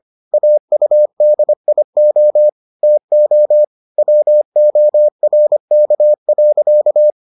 Morse code, heard as a keyed tone, about 25 wpm, 600 Hz. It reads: AUDIO TO WORK.